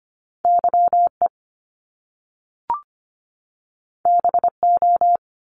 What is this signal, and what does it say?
Morse code, heard as a keyed tone, about 25 wpm, 700 Hz. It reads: Y E B O
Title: Top Two Letter Combinations in Words 25wpm